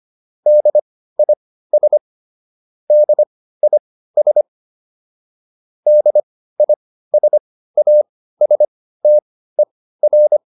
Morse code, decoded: DIS DIS DISASTER

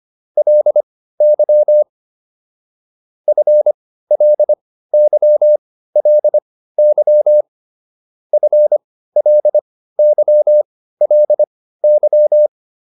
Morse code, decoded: LY FLY LY FLY LY

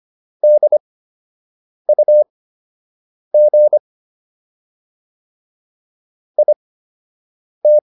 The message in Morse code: DUG IT

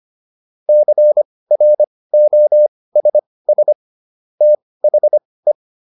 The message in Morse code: CROSS THE

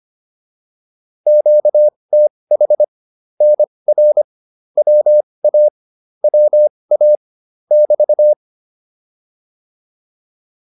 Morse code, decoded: QTH NR WA WA <BT>

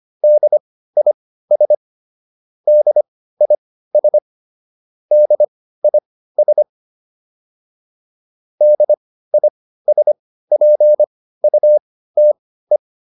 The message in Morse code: DIS DIS DIS DISPUTE

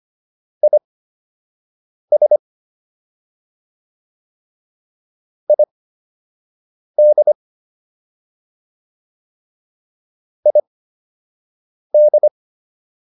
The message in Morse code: IS ID ID

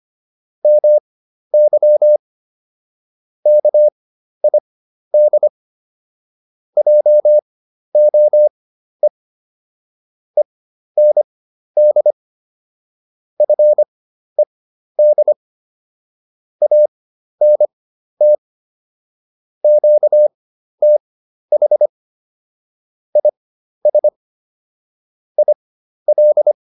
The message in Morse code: MY KID JOE END FED ANT QTH IS IL

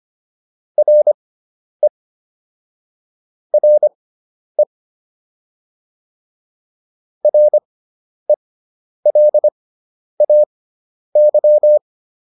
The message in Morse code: RE RE RELAY

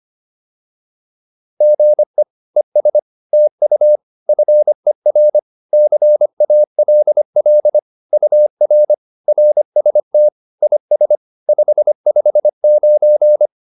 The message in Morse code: GE ES TU FER CALL UR RST IS 559

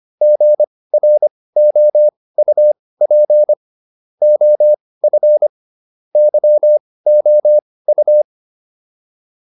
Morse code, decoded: GROUP OF YOU

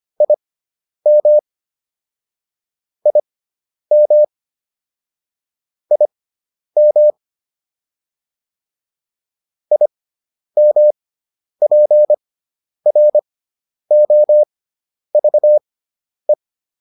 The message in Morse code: IM IM IM IMPROVE

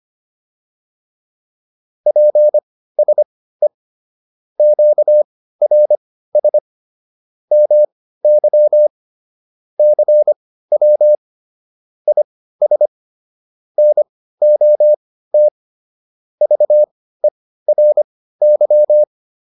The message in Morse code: PSE QRS MY CW IS NOT VERY

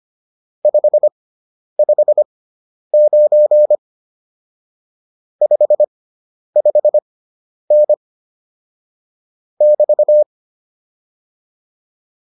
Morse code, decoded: 559 55N <BT>

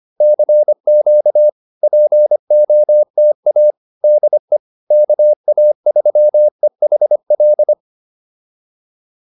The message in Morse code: CQ POTA DE KA3EHL

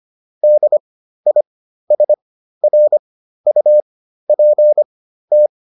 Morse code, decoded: DISRUPT

